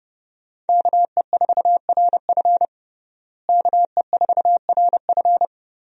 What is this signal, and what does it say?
Morse code, heard as a keyed tone, about 31 wpm, 700 Hz. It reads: KE4RF KE4RF
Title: USA Call Signs - 5 Characters - Repeated 3x - 30wpm